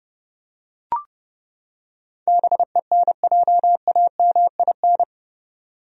Morse code, heard as a keyed tone, about 30 wpm, 700 Hz. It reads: BENJAMIN
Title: USA Common Names - 30wpm